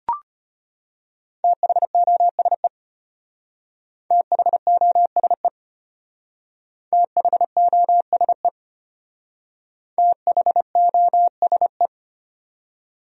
Morse code, heard as a keyed tone, about 32 wpm, 700 Hz. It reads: THOSE THOSE THOSE THOSE